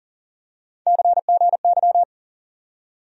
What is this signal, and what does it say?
Morse code, heard as a keyed tone, about 38 wpm, 700 Hz. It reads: CGY
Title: USA Call Sign Postfixes - 3 Letters 40wpm